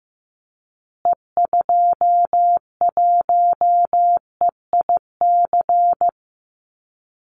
Morse code, heard as a keyed tone, about 15 wpm, 700 Hz. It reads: E21EIC
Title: Call Signs Repeated 3x - 15wpm